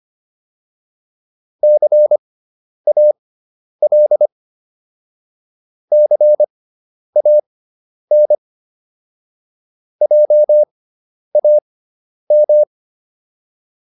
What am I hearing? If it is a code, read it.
Morse code, heard as a keyed tone, about 25 wpm, spaced as for 10 wpm, 600 Hz. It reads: CAL CAN JAM